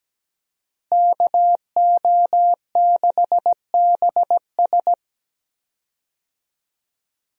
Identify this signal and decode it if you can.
Morse code, heard as a keyed tone, about 17 wpm, 700 Hz. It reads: KO6BS
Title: USA Call Signs - 5 Characters - Repeated 3x - 17wpm